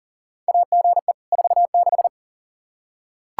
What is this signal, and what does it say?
Morse code, heard as a keyed tone, about 40 wpm, 700 Hz. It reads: AGE 46
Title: Practice QSO - Exchange 3 - More Info - 40wpm